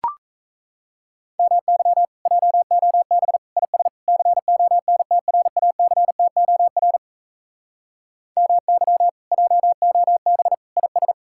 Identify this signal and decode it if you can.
Morse code, heard as a keyed tone, about 40 wpm, 700 Hz. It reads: MY JOB IS CONTRACTOR MY JOB IS